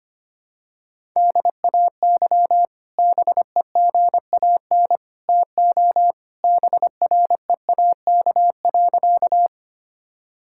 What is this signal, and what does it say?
Morse code, heard as a keyed tone, about 25 wpm, 700 Hz. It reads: DAY BEGAN TO BREAK.
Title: Sentences from Top 1000 Words - Review Part 6 - 25wpm